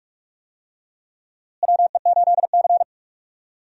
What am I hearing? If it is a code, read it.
Morse code, heard as a keyed tone, about 46 wpm, 700 Hz. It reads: WE8C
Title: USA Call Signs - 4 Characters - Repeated 2x - 45wpm